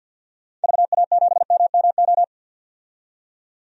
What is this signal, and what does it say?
Morse code, heard as a keyed tone, about 46 wpm, 700 Hz. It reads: VA7MMO
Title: DXCC-100 Call Signs - 6 Characters - 50wpm